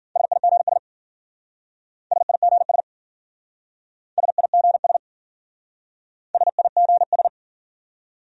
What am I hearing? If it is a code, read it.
Morse code, heard as a keyed tone, about 49 wpm, 700 Hz. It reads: SIGS SIGS SIGS SIGS
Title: Speed-Racing Abbreviations - 40wpm